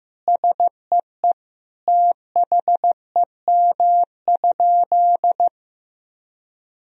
Morse code, heard as a keyed tone, about 16 wpm, 700 Hz. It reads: SEE THEM?
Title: Sentences from Top 1000 Words - Review Part 1 - 15wpm